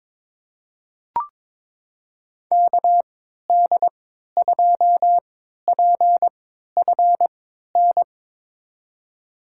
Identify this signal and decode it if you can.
Morse code, heard as a keyed tone, about 21 wpm, spaced as for 12 wpm, 700 Hz. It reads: KD2PFN
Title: USA Call Signs - 3x Character Spacing - 22wpm